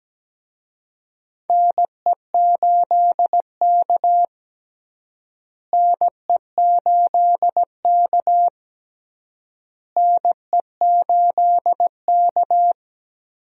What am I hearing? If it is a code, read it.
Morse code, heard as a keyed tone, about 17 wpm, 700 Hz. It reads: NE8K NE8K NE8K
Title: USA Call Signs - 4 Characters - Repeated 3x - 17wpm